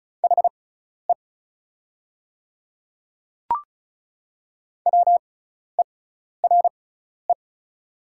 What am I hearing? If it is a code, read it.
Morse code, heard as a keyed tone, about 34 wpm, spaced as for 12 wpm, 690 Hz. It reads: HE WERE
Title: 6x Character Spacing - Top 200 Words - 35wpm